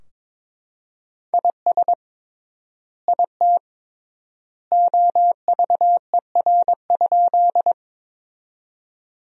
Morse code, heard as a keyed tone, about 21 wpm, 700 Hz. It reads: IS IT OVER?
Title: 3x Word-Spacing - Sentences from Top 100 Words - 22wpm